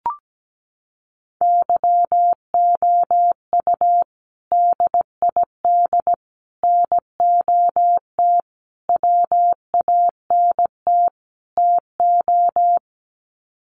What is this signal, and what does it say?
Morse code, heard as a keyed tone, about 17 wpm, 700 Hz. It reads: YOU DID NOT WANT TO
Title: Sets of 5 Words Encore2 17wpm